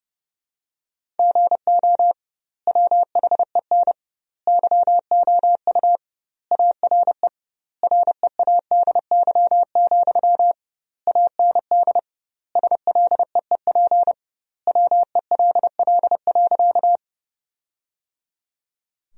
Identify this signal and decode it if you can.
Morse code, heard as a keyed tone, about 30 wpm, 700 Hz. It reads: GO WHEN YOU ARE READY, AND SLEEP WELL.
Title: 2x Word-Spacing - Sentences from Top 500 Words - 30wpm